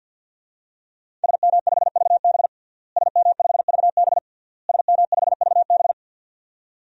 Morse code, heard as a keyed tone, about 51 wpm, 700 Hz. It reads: SM5VB SM5VB SM5VB
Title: DXCC-100 Call Signs - 5 Characters - Repeated 3x - 50wpm